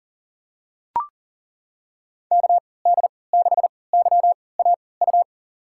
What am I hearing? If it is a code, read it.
Morse code, heard as a keyed tone, about 40 wpm, spaced as for 23 wpm, 700 Hz. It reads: KD6YAU
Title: USA Call Signs - 6 Characters - 3x Character Spacing - 40wpm